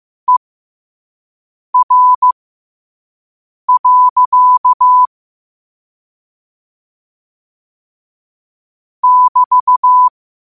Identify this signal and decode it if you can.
Morse code, heard as a keyed tone, about 15 wpm, spaced as for 5 wpm, 1000 Hz. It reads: ER. <BT>
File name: qotdfiles / 5wpm